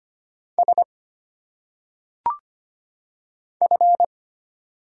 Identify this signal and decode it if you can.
Morse code, heard as a keyed tone, about 25 wpm, 700 Hz. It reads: S F
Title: Single Letters - Phonetic Pronunciation 25wpm